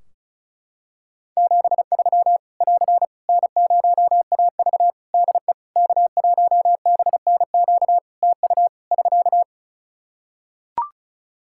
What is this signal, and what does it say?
Morse code, heard as a keyed tone, about 36 wpm, 700 Hz. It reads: 73 <AR> N0AV DE K1BNQ TU <SK>
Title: Practice QSO - Exchange 4 - Ending - 35wpm